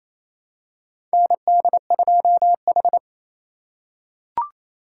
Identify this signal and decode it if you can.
Morse code, heard as a keyed tone, about 28 wpm, 700 Hz. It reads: ND2H
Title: USA Call Signs - 4 Characters - 2x Character Spacing - 28wpm